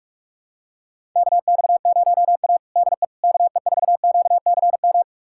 Morse code, heard as a keyed tone, about 46 wpm, 690 Hz. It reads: KX0A DE KE4QCM
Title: USA Callsign DE Callsign 45wpm